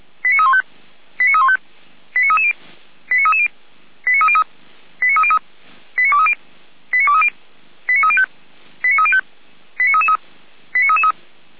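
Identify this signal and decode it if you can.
Five-tone selective call (ZVEI), heard as quick runs of five tones, tones of about 70 ms. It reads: 89315 89315 89330 89330 89303 89303 89230 89230 89305 89305 89333 89333
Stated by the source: ZVEI five-tone selective call received via SDR.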